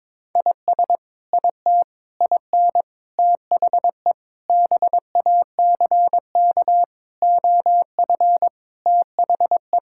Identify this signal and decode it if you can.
Morse code, heard as a keyed tone, about 22 wpm, 700 Hz. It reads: IS IT IN THE BACK OF THE
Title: Sentences from Top 1000 Words - Review Part 5 - 22wpm